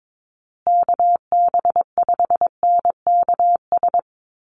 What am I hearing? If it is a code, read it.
Morse code, heard as a keyed tone, about 22 wpm, 700 Hz. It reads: KB5NKS
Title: USA Call Signs Repeated 3x 22wpm